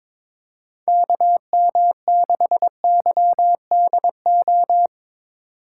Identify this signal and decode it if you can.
Morse code, heard as a keyed tone, about 21 wpm, 700 Hz. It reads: KM6YDO
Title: USA Call Signs - 6 Characters - 4x Character Spacing - 22wpm